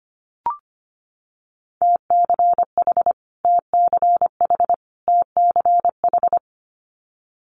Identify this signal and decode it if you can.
Morse code, heard as a keyed tone, about 25 wpm, 700 Hz. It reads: TCH TCH TCH
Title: Multi-letter Phonemes Repeated 3x 25wpm